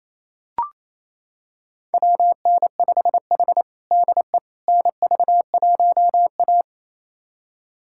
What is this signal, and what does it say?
Morse code, heard as a keyed tone, about 27 wpm, 700 Hz. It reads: WN5H DE NV1A
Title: USA Callsign DE Callsign - 4 Characters - 28wpm